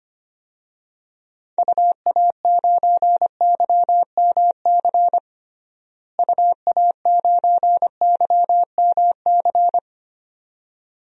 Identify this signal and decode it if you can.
Morse code, heard as a keyed tone, about 25 wpm, 700 Hz. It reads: UA9YMC UA9YMC
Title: DXCC-100 Call Signs - 6 Characters - Repeated 2x - 25wpm